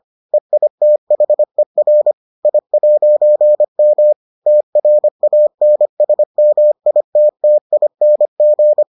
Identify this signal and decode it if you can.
Morse code, heard as a keyed tone, about 25 wpm, 600 Hz. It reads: EITHER I'M TRANSMITTING